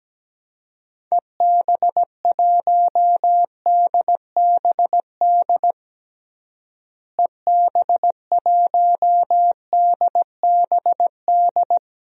Morse code, heard as a keyed tone, about 17 wpm, 700 Hz. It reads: EB1DBD EB1DBD
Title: DXCC-100 Call Signs - 6 Characters - Repeated 3x - 17wpm